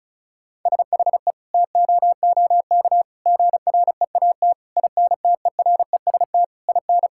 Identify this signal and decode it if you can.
Morse code, heard as a keyed tone, about 36 wpm, 700 Hz. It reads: SHE TOOK GREAT INTEREST IN
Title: Sentences from Top 1000 Words - Review Part 6 - 35wpm